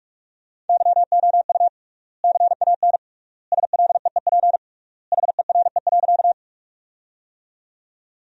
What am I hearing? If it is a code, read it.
Morse code, heard as a keyed tone, about 46 wpm, 700 Hz. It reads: YOU CAN SLEEP HERE.
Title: 3x Word-Spacing - Sentences from Top 500 Words - 45wpm